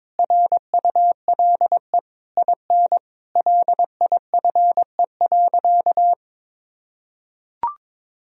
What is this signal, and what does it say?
Morse code, heard as a keyed tone, about 22 wpm, 700 Hz. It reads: RULE IN LIFE.
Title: Sentences from Top 1000 Words - Review Part 2 - 22wpm